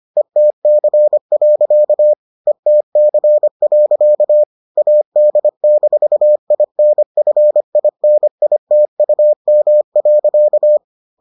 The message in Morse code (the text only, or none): ETC. ETC. AD-INFINITUM.